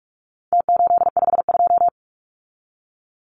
Morse code, heard as a keyed tone, about 44 wpm, 700 Hz. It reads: T852
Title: Numbers - Contest Style - 4-Digit 45wpm